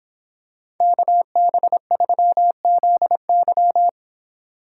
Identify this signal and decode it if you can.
Morse code, heard as a keyed tone, about 24 wpm, 700 Hz. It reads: KB3ZY